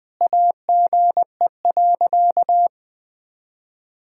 Morse code, heard as a keyed tone, about 20 wpm, 700 Hz. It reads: AGE.
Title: Sentences from Top 1000 Words - Review Part 3 - 20wpm